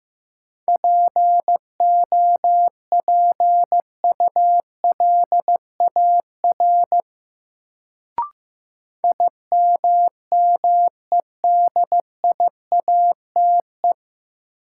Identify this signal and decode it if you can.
Morse code, heard as a keyed tone, about 15 wpm, 700 Hz. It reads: POPULAR IMMEDIATE
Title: Sentences from Top 1200 Words - New Word Drill Alt - 15wpm